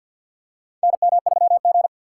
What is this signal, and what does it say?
Morse code, heard as a keyed tone, about 47 wpm, 700 Hz. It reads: NM3G
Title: USA Call Signs - 4 Characters - 3x Character Spacing - 50wpm